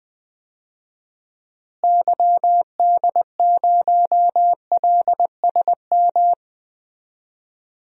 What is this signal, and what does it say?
Morse code, heard as a keyed tone, about 20 wpm, 700 Hz. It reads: YD0LSM
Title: DXCC-100 Call Signs - 6 Characters - Repeated 3x - 20wpm